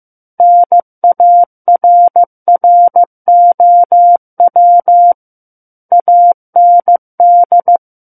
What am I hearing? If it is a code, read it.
Morse code, heard as a keyed tone, about 15 wpm, 700 Hz. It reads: NARROW AND
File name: Book One / 06-Chapter 6 - 15 WPM